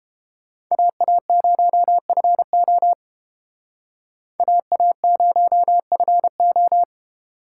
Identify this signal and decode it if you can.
Morse code, heard as a keyed tone, about 30 wpm, 700 Hz. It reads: AA0FO AA0FO